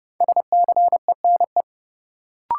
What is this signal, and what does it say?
Morse code, heard as a keyed tone, about 32 wpm, 700 Hz. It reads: SCENE
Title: Sentences from Top 1100 Words - New Word Drill Alt - 30wpm